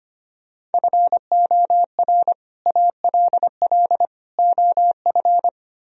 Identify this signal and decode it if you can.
Morse code, heard as a keyed tone, about 33 wpm, 700 Hz. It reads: FOR ALL OF